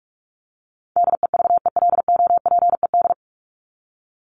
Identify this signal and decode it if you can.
Morse code, heard as a keyed tone, about 44 wpm, 700 Hz. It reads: DEVELOPED
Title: ED Suffix - Top 100 Words with Base Words 45wpm